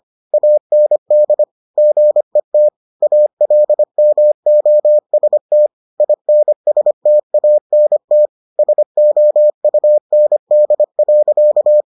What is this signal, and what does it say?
Morse code, heard as a keyed tone, about 25 wpm, 600 Hz. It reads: AND GET ALMOST INSTANT SOUND.